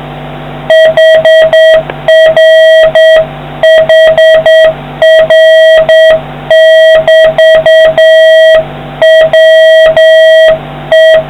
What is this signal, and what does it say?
Morse code, heard as a keyed tone, about 7 wpm, 1900 Hz, with noise. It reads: HRHR<BT>WE